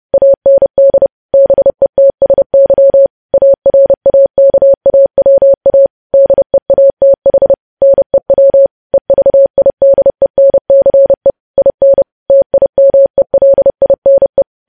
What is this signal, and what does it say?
Morse code, heard as a keyed tone, about 30 wpm, 550 Hz. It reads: AND BETSY ARAKAWA DEATH NEW EVIDENCE IN TIMELINE
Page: news headlines in morse code at 30 wpm